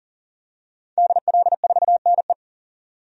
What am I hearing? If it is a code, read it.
Morse code, heard as a keyed tone, about 40 wpm, 700 Hz. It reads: DL4NE
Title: DXCC-100 Call Signs - 5 Characters - 40wpm